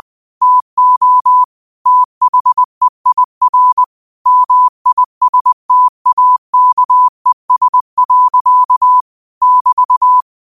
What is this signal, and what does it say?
Morse code, heard as a keyed tone, about 20 wpm, 1000 Hz. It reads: TO THEIR MISTAKES. <BT>
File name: qotdfiles / 20wpm